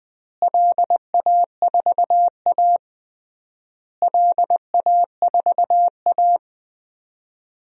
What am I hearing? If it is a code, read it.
Morse code, heard as a keyed tone, about 20 wpm, 700 Hz. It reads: LA4A LA4A
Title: DXCC-100 Call Signs - 4 Characters - Repeated 3x - 20wpm